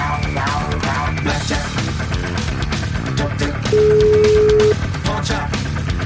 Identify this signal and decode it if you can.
Ringback tone (Europe, 425 Hz), with noise.